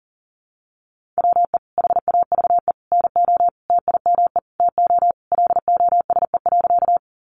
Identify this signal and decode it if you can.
Morse code, heard as a keyed tone, about 40 wpm, 700 Hz. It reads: WE HAVE NO TIME TO LOSE.
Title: Sentences from Top 1100 Words 40wpm